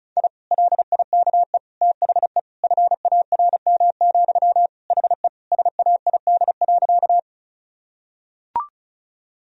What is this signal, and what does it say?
Morse code, heard as a keyed tone, about 36 wpm, 700 Hz. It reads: I LIKE THE FARM, HE SAID.
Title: Sentences from Top 1000 Words - Review Part 2 - 35wpm